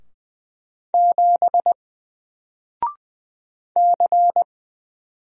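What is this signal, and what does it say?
Morse code, heard as a keyed tone, about 20 wpm, 700 Hz. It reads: 7 C
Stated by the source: Single letter and number Morse drill at 20 wpm, from N4LCD.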